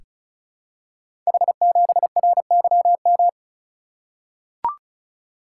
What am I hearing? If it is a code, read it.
Morse code, heard as a keyed tone, about 36 wpm, 700 Hz. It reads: H7RYM
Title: Code Groups - 5 Letter:Numbers - 35wpm